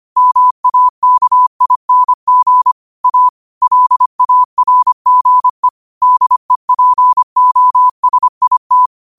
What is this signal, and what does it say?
Morse code, heard as a keyed tone, about 25 wpm, 1000 Hz. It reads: MAKING A LARGE DEPOSIT
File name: qotdfiles / 25wpm